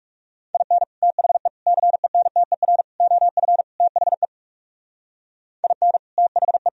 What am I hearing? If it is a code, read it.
Morse code, heard as a keyed tone, about 36 wpm, 700 Hz. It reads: IN THE CENTER OF THE IN THE
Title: Speed-Racing - Sets of 5 Words 30wpm